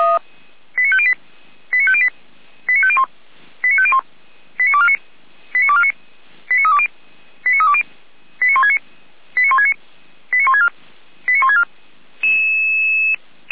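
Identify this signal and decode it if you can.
Five-tone selective call (ZVEI), heard as quick runs of five tones, tones of about 70 ms. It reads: 89508 89508 89501 89501 89249 89249 89320 89320 89169 89169 89164 89164
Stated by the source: ZVEI five-tone selective call 89171 received by SDR.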